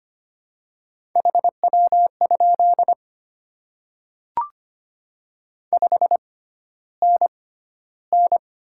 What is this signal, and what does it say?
Morse code, heard as a keyed tone, about 25 wpm, spaced as for 13 wpm, 700 Hz. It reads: HW? 5NN
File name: Top 20 QSO Elements - 6x Character Spacing - 25wpm